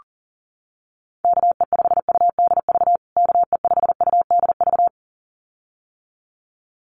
Morse code, heard as a keyed tone, about 40 wpm, 700 Hz. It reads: KE5UDV KE5UDV
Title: USA Call Signs Repeated 2x 40wpm